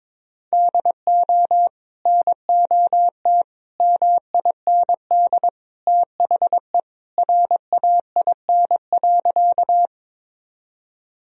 Morse code, heard as a keyed tone, about 21 wpm, 700 Hz. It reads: DO NOT MIND THE RAIN.
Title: Sentences from Top 1000 Words - Review Part 1 - 22wpm